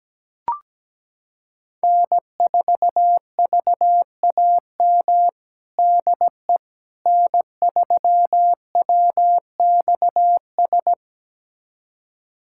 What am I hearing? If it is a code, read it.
Morse code, heard as a keyed tone, about 17 wpm, 700 Hz. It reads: N4VAM DE N3WXS
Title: USA Callsign DE Callsign - 5 Characters - 17wpm